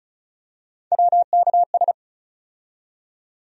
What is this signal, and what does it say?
Morse code, heard as a keyed tone, about 36 wpm, 700 Hz. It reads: WKS
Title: USA Call Sign Postfixes - 3 Letters 35wpm